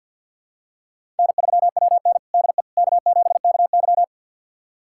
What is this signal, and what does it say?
Morse code, heard as a keyed tone, about 48 wpm, 690 Hz. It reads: N3WN DE K7KY
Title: USA Callsign DE Callsign - 4 Characters - 50wpm